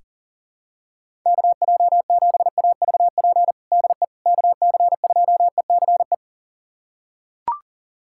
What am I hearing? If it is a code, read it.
Morse code, heard as a keyed tone, about 40 wpm, 700 Hz. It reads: KJ7AVP DE KC2ECE
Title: USA Callsign DE Callsign 40wpm